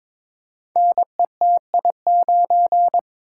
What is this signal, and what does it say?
Morse code, heard as a keyed tone, about 22 wpm, 700 Hz. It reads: NETI9
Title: Code Groups - 5 Letter:Numbers - 22wpm